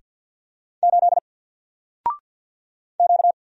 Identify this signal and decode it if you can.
Morse code, heard as a keyed tone, about 50 wpm, 700 Hz. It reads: 8 Q
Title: Single Letter-Number 50wpm